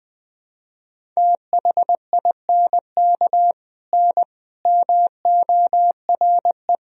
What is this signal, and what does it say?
Morse code, heard as a keyed tone, about 21 wpm, 700 Hz. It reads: THINK N MORE
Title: Sentences from Top 200 Words and Abbreviations 20wpm